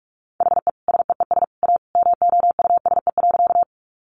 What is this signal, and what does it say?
Morse code, heard as a keyed tone, about 44 wpm, 700 Hz. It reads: HE SEES A MOUSE.
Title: Easy Phrases 45wpm